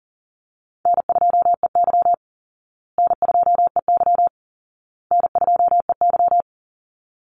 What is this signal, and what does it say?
Morse code, heard as a keyed tone, about 40 wpm, 700 Hz. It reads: N2EY N2EY N2EY
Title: Easy Call Signs Repeated 3x - 40wpm